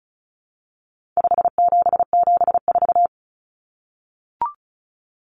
Morse code, heard as a keyed tone, about 34 wpm, 700 Hz. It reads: 5774
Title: Common Cut Numbers - 4-Digit 35wpm